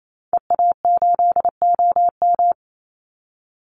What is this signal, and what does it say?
Morse code, heard as a keyed tone, about 28 wpm, 700 Hz. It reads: EA8OM
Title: Call Signs Repeated 2x - 28wpm